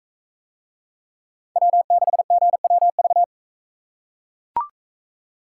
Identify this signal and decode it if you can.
Morse code, heard as a keyed tone, about 41 wpm, 700 Hz. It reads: W6GWV